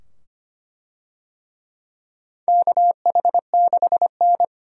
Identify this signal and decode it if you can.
Morse code, heard as a keyed tone, about 25 wpm, 700 Hz. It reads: KH6N
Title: DXCC-100 Call Signs - 4 Characters - 25wpm